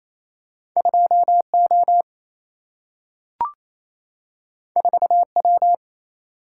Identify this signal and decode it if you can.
Morse code, heard as a keyed tone, about 28 wpm, 700 Hz. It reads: 2 O 4 W
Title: Single-Letter Single-Number 28wpm